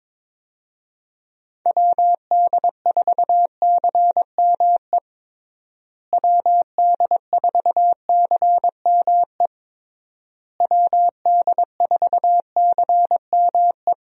Morse code, heard as a keyed tone, about 21 wpm, 700 Hz. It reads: WD4CME WD4CME WD4CME
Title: USA Call Signs - 6 Characters - Repeated 3x - 22wpm